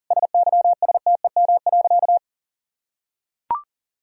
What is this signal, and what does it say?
Morse code, heard as a keyed tone, about 40 wpm, 700 Hz. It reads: SYSTEM.
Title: Sentences from Top 1000 Words - Review Part 6 - 40wpm